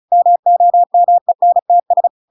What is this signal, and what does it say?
Morse code, heard as a keyed tone, about 34 wpm, 700 Hz. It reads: MOMENTS
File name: Pinocchio / Pinocchio ch5 35 35 ik1hge